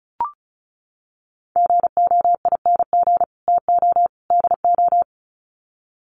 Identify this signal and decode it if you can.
Morse code, heard as a keyed tone, about 34 wpm, 700 Hz. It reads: GOING TO DO
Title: Sets of 3 Words 35wpm